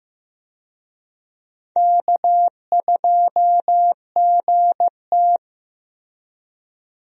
Morse code, heard as a keyed tone, about 15 wpm, 700 Hz. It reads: K2GT
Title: USA Call Signs - 4 Characters - Repeated 3x - 15wpm